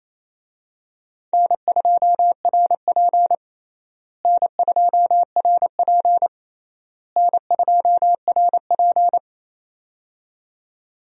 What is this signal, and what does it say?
Morse code, heard as a keyed tone, about 28 wpm, 700 Hz. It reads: N2RP N2RP N2RP
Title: USA Call Signs - 4 Characters - Repeated 3x - 28wpm